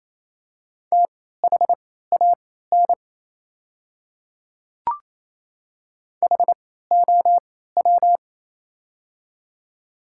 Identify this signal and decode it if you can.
Morse code, heard as a keyed tone, about 27 wpm, spaced as for 16 wpm, 700 Hz. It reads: THAN HOW
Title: 3x Character Spacing - Top 200 Words - 28wpm